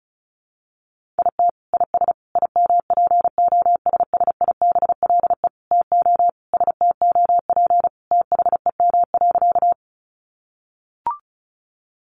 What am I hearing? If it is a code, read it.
Morse code, heard as a keyed tone, about 34 wpm, 700 Hz. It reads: IT IS IMPOSSIBLE TO STOP THEM.
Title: Sentences from Top 1100 Words 35wpm